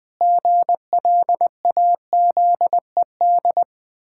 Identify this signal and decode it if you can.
Morse code, heard as a keyed tone, about 21 wpm, 700 Hz. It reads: GLAZED
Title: Pangram Challenge v2 - 20wpm